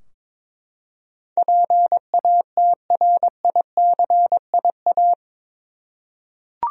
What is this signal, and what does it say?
Morse code, heard as a keyed tone, about 22 wpm, 700 Hz. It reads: PATRICIA
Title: USA Common Names - 22wpm